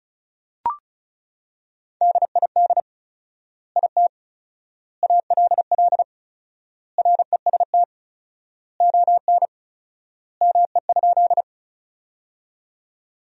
Morse code, heard as a keyed tone, about 34 wpm, 700 Hz. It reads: DID IT ALL REST ON ME?
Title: 4x Word-Spacing - Sentences from Top 500 Words - 35wpm